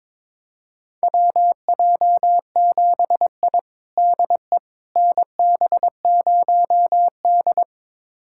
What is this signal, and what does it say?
Morse code, heard as a keyed tone, about 22 wpm, 700 Hz. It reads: WJ7I DE NB0D
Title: USA Callsign DE Callsign - 4 Characters - 22wpm